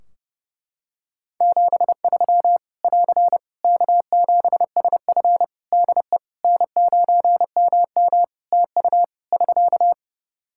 Morse code, heard as a keyed tone, about 32 wpm, 700 Hz. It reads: 73 <AR> K7SF DE N9MM TU <SK>
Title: K1USN SST QSO Practice 30wpm